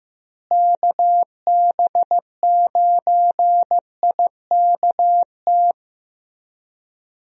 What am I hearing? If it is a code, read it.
Morse code, heard as a keyed tone, about 15 wpm, 700 Hz. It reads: KB9IKT
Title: USA Call Signs - 6 Characters - Repeated 3x - 15wpm